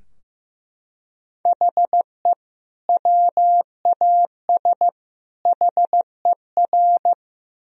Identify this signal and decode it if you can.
Morse code, heard as a keyed tone, about 14 wpm, 700 Hz. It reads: HE WAS HER
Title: Sentences from Top 1000 Words - Review Part 3 - 15wpm